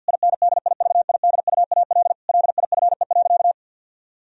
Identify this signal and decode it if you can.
Morse code, heard as a keyed tone, about 50 wpm, 700 Hz. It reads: INDIVIDUAL LIFE.
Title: Sentences from Top 1000 Words - Review Part 4 - 50wpm